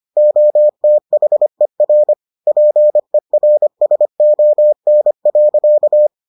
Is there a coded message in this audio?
Yes: Morse code from 0.2 to 6.1 s, heard as a keyed tone, about 25 wpm, 600 Hz. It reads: OTHER PERSON.